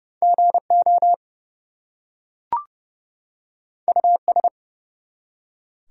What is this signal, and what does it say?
Morse code, heard as a keyed tone, about 29 wpm, 700 Hz. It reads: GO US
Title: Two Letter Words - 30wpm